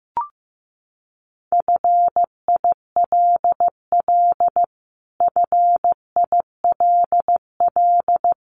Morse code, heard as a keyed tone, about 15 wpm, 700 Hz. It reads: FILL FILL
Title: Top 500 Words Repeated 15wpm